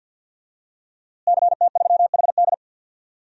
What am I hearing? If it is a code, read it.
Morse code, heard as a keyed tone, about 47 wpm, 700 Hz. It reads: CT3HD
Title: DXCC-100 Call Signs - 5 Characters - Repeated 2x - 50wpm